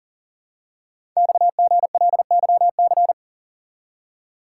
Morse code, heard as a keyed tone, about 39 wpm, 700 Hz. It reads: XGLYC
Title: Code Groups - 5 Letters - 40wpm